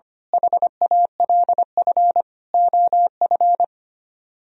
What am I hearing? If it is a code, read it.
Morse code, heard as a keyed tone, about 26 wpm, 700 Hz. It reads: HALF OF
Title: Speed-Racing - Sets of 5 Words 25wpm